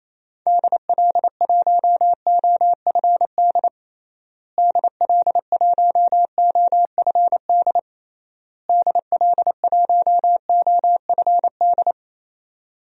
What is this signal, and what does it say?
Morse code, heard as a keyed tone, about 28 wpm, 700 Hz. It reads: DL1OFD DL1OFD DL1OFD
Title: DXCC-100 Call Signs - 6 Characters - Repeated 3x - 28wpm